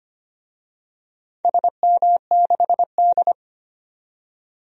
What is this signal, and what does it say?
Morse code, heard as a keyed tone, about 25 wpm, 700 Hz. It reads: SM6D
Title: DXCC-100 Call Signs - 4 Characters - 25wpm